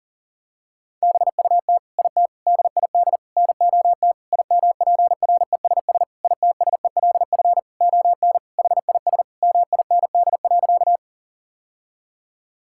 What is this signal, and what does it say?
Morse code, heard as a keyed tone, about 40 wpm, 700 Hz. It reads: BUT IT DID NOT IMPRESS ITSELF ON HIS MIND.
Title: Sentences from Top 1600 Words 40wpm